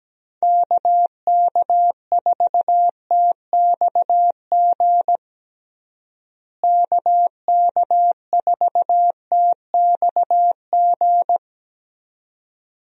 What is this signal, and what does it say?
Morse code, heard as a keyed tone, about 17 wpm, 700 Hz. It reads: KK4TXG KK4TXG
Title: USA Call Signs - 6 Characters - Repeated 2x - 17wpm